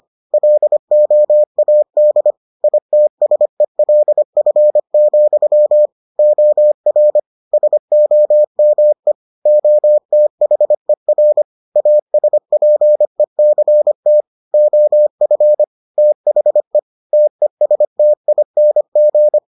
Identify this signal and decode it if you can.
Morse code, heard as a keyed tone, about 25 wpm, 600 Hz. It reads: LOAD ITSELF, OR SOME OTHER ASPECT OF THE TESTING